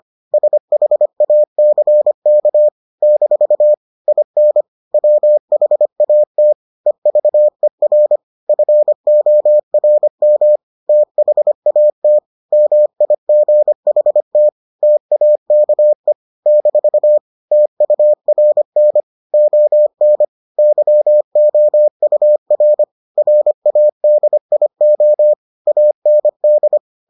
Morse code, heard as a keyed tone, about 25 wpm, 600 Hz. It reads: SHACK - IN WHAT EVER FORM THAT MIGHT TAKE - TURN ON YOUR RADIO AND